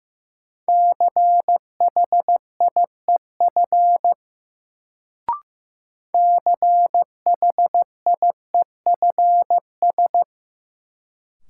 Morse code, heard as a keyed tone, about 15 wpm, 700 Hz. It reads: CHIEF CHIEFS
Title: Sets of 5 Words Encore - New Word Drill - 15wpm